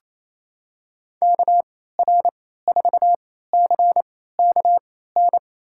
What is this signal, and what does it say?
Morse code, heard as a keyed tone, about 27 wpm, spaced as for 16 wpm, 700 Hz. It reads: KR4CKN
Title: USA Call Signs - 3x Character Spacing - 28wpm